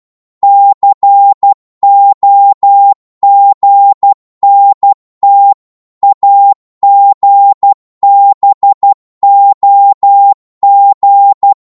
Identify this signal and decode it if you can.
Morse code, heard as a keyed tone, about 12 wpm, 800 Hz. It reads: COGNT AGBOG
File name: Morse / ltr-12wpm